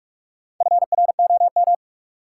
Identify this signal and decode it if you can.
Morse code, heard as a keyed tone, about 43 wpm, 700 Hz. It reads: FROM
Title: Sentences from Top 100 Words - New Word Drill - 45wpm